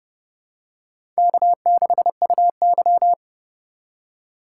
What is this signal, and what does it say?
Morse code, heard as a keyed tone, about 29 wpm, 700 Hz. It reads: K6UY
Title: USA Call Signs - 4x Character Spacing - 30wpm